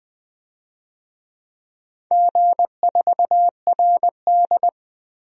Morse code, heard as a keyed tone, about 21 wpm, 700 Hz. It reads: G4RD
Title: DXCC-100 Call Signs - 4 Characters - 20wpm